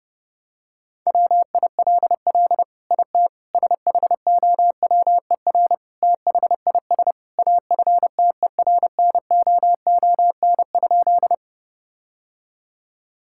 Morse code, heard as a keyed tone, about 30 wpm, 700 Hz. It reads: WILL IT SHOWER THIS AFTERNOON?
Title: Sentences from Top 1600 Words 30wpm